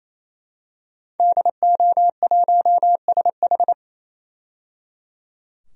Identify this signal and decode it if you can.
Morse code, heard as a keyed tone, about 28 wpm, 700 Hz. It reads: DO1SH
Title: DXCC-100 Call Signs - 5 Characters - 28wpm